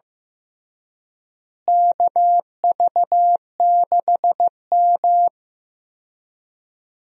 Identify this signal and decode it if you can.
Morse code, heard as a keyed tone, about 15 wpm, 700 Hz. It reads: KV6M
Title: USA Call Signs - 4 Characters - Repeated 2x - 15wpm